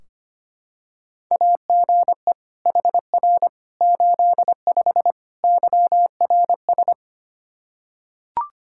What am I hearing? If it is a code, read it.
Morse code, heard as a keyed tone, about 25 wpm, 700 Hz. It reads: AGE HR 85 YRS
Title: Practice QSO - Exchange 3 - More Info - 25wpm